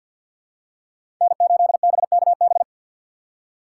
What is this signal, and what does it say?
Morse code, heard as a keyed tone, about 46 wpm, 700 Hz. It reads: N8BKB
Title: USA Call Signs - 5 Characters - 6x Character Spacing - 50wpm